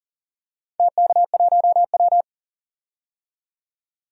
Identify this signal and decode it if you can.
Morse code, heard as a keyed tone, about 42 wpm, 700 Hz. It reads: TK1W
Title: DXCC-100 Call Signs - 4 Characters - 40wpm